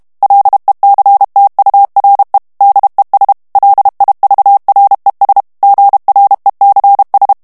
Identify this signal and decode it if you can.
Morse code, heard as a keyed tone, about 32 wpm, 800 Hz. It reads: LECTURE DES LIVRES GRECS
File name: FRANCAIS TEXTE CLAIR 2 VITESSE 30 MOTS MINUTES